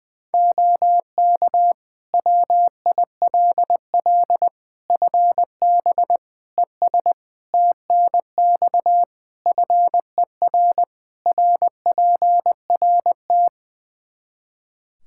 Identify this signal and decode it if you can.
Morse code, heard as a keyed tone, about 20 wpm, 700 Hz. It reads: OK WILL FB ES TNX FER RPRT
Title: Practice QSO - Exchange 2 - Info - 20wpm